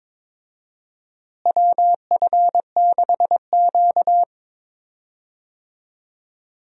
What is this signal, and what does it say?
Morse code, heard as a keyed tone, about 21 wpm, 700 Hz. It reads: WF6Q
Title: USA Call Signs - 4 Characters - Repeated 3x - 22wpm